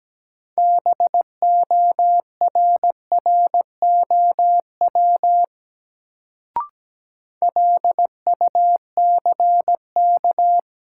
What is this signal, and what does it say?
Morse code, heard as a keyed tone, about 17 wpm, 700 Hz. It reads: BORROW LUCK
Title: Sentences from Top 1500 Words - New Word Drill Alt - 17wpm